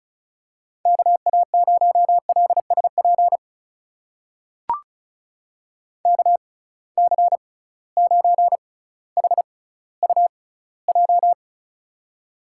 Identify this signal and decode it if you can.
Morse code, heard as a keyed tone, about 36 wpm, spaced as for 18 wpm, 700 Hz. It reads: KA0LSP KC9HUJ
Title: USA Call Signs - 6 Characters - 6x Character Spacing - 35wpm